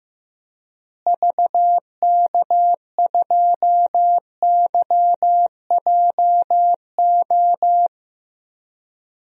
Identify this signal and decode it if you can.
Morse code, heard as a keyed tone, about 15 wpm, 700 Hz. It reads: VK2YJO
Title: DXCC-100 Call Signs - 6 Characters - Repeated 2x - 15wpm